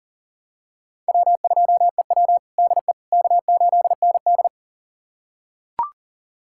Morse code, heard as a keyed tone, about 40 wpm, 700 Hz. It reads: W2EW DE K8ND
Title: USA Callsign DE Callsign - 4 Characters - 40wpm